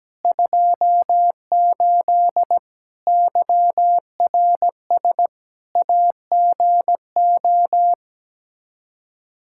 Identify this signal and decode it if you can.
Morse code, heard as a keyed tone, about 17 wpm, 700 Hz. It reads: 28 YRS AGO
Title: Practice QSO - Exchange 3 - More Info - 17wpm